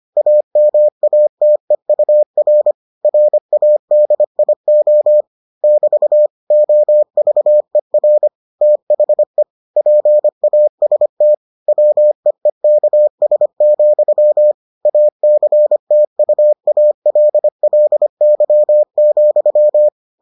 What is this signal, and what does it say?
Morse code, heard as a keyed tone, about 25 wpm, 600 Hz. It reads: AMATEUR RADIO <BT> OVER THE PAST WEEKS, ACTUALLY,